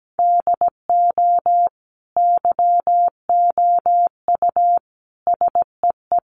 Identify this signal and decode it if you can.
Morse code, heard as a keyed tone, about 17 wpm, 700 Hz. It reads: DO YOU SEE
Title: Sentences from Top 500 Words 17wpm